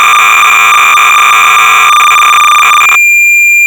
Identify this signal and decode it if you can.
Cassette data (Kansas City type).
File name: sorcerer / 300baud / Z80 Disassembler-(1979)(Victor Tolomei)(Quality Software)-BASIC-Sorcerer-300baud